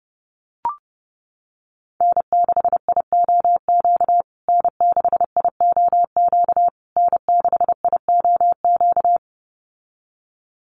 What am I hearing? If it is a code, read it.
Morse code, heard as a keyed tone, about 30 wpm, 700 Hz. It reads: N6IOQ N6IOQ N6IOQ
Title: USA Call Signs Repeated 3x 30wpm